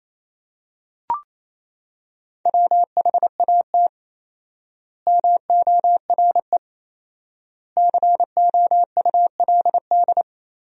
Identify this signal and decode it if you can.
Morse code, heard as a keyed tone, about 28 wpm, 700 Hz. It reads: WHAT MORE COULD
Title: 4x Word-Spacing - Sentences from Top 200 Words - 28wpm